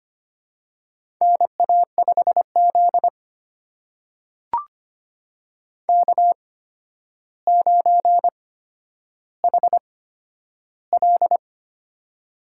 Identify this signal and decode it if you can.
Morse code, heard as a keyed tone, about 25 wpm, spaced as for 11 wpm, 700 Hz. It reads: NA5Z K9HL
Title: USA Call Signs - 4 Characters - 8x Character Spacing - 25wpm